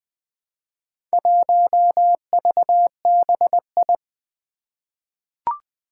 Morse code, heard as a keyed tone, about 20 wpm, 700 Hz. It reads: 1VBI
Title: Four Challenging Letter-Number Combinations 20wpm